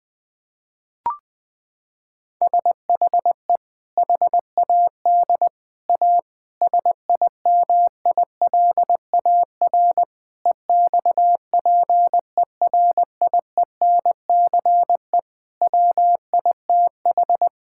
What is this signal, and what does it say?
Morse code, heard as a keyed tone, about 21 wpm, 700 Hz. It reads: SHE HAD A SIMILAR EXPERIENCE WITH
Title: Sentences from Top 1000 Words - Review Part 5 - 20wpm